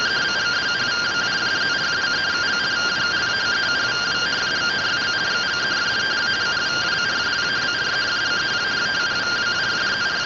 Two-tone teleprinter signal, about 170 Hz shift.